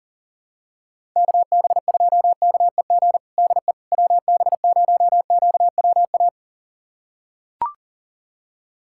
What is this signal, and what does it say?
Morse code, heard as a keyed tone, about 40 wpm, 700 Hz. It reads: KB2KEG DE WB0QWA
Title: USA Callsign DE Callsign 40wpm